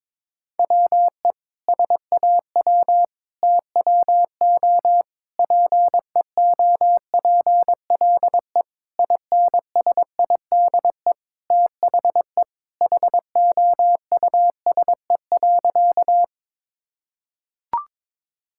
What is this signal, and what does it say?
Morse code, heard as a keyed tone, about 22 wpm, 700 Hz. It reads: WE SAW TWO PEOPLE INSIDE THE HOUSE.
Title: Sentences from Top 1000 Words - Review Part 6 - 22wpm